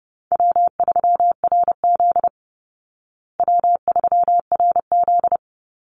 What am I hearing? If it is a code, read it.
Morse code, heard as a keyed tone, about 30 wpm, 700 Hz. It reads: W3RZ W3RZ
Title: Easy Call Signs Repeated 2x - 30wpm